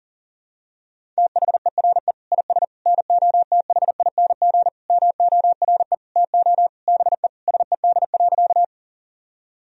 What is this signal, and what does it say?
Morse code, heard as a keyed tone, about 40 wpm, 700 Hz. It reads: THERE IS NOTHING MORE TO BE SED.
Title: Sentences from Top 500 Words and Abbreviations 40wpm